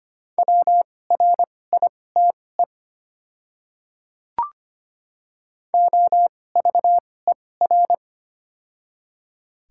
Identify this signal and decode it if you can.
Morse code, heard as a keyed tone, about 25 wpm, spaced as for 18 wpm, 700 Hz. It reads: WRITE OVER
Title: 2x Character Spacing - Top 300 Words - 25wpm